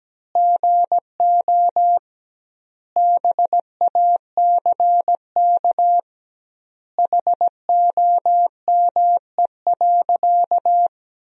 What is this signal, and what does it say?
Morse code, heard as a keyed tone, about 17 wpm, 700 Hz. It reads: GO BACK HOME.
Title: 2x Word-Spacing - Sentences from Top 500 Words - 17wpm